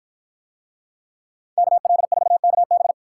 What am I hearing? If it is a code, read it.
Morse code, heard as a keyed tone, about 53 wpm, 700 Hz. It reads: KL4KB